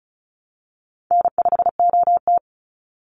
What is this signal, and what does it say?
Morse code, heard as a keyed tone, about 34 wpm, 700 Hz. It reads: N5OT
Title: Call Signs Repeated 3x - 35wpm